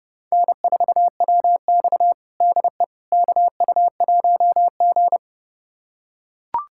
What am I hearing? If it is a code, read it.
Morse code, heard as a keyed tone, about 30 wpm, 700 Hz. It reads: N4WX DE KU1G
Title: USA Callsign DE Callsign - 4 Characters - 30wpm